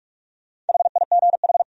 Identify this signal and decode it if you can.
Morse code, heard as a keyed tone, about 58 wpm, 700 Hz. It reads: HIGH